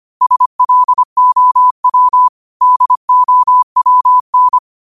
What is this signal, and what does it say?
Morse code, heard as a keyed tone, about 25 wpm, 1000 Hz. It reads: SLOW DOWN